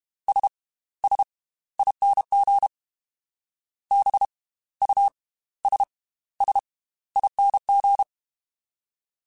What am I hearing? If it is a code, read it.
Morse code, heard as a keyed tone, about 32 wpm, spaced as for 17 wpm, 800 Hz. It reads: SSING BUSSING
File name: Slow ing @30